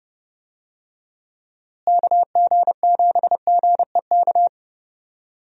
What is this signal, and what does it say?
Morse code, heard as a keyed tone, about 31 wpm, 700 Hz. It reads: KG7GEK
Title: USA Call Signs - 6 Characters - Repeated 2x - 30wpm